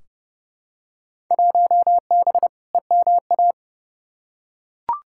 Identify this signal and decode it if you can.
Morse code, heard as a keyed tone, about 30 wpm, 700 Hz. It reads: 1B EMA
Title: Field Day Exchanges - 30wpm